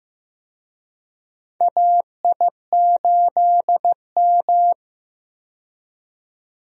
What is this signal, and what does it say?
Morse code, heard as a keyed tone, about 15 wpm, 700 Hz. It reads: AI8M
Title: USA Call Signs - 4 Characters - 15wpm